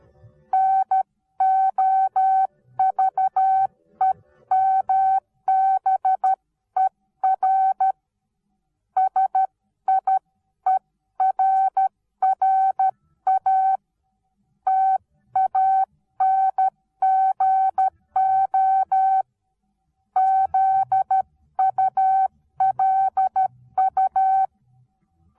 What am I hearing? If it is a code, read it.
Morse code, heard as a keyed tone, about 13 wpm, 760 Hz. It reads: NOVEMBER SIERRA TANGO ZULU
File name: Onlinelehrgang / morsen / DJ4UF Morsen19